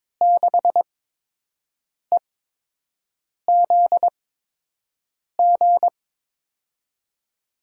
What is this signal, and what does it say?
Morse code, heard as a keyed tone, about 22 wpm, 700 Hz. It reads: 6 E Z G
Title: USA Call Signs - 6 Characters - 8x Character Spacing - 22wpm